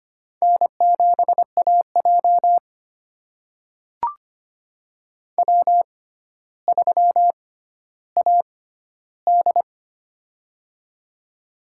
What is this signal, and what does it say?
Morse code, heard as a keyed tone, about 25 wpm, spaced as for 13 wpm, 700 Hz. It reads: N7AJ W3AD
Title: USA Call Signs - 6x Character Spacing - 25wpm